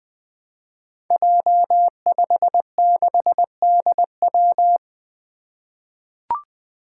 Morse code, heard as a keyed tone, about 20 wpm, 700 Hz. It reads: J56DW
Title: Five Challenging Letter-Number Combinations 20wpm